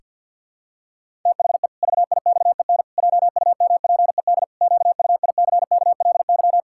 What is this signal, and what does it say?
Morse code, heard as a keyed tone, about 49 wpm, 700 Hz. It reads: THE VIXEN JUMPED QUICKLY
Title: Pangram Challenge v2 - 50wpm